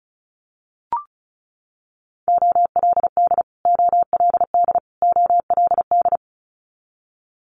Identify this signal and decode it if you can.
Morse code, heard as a keyed tone, about 36 wpm, 700 Hz. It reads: OLD OLD OLD
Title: Multi-letter Phonemes Repeated 3x 35wpm